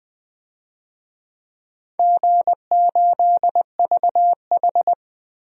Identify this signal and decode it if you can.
Morse code, heard as a keyed tone, about 20 wpm, 700 Hz. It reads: G8VH
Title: DXCC-100 Call Signs - 4 Characters - 20wpm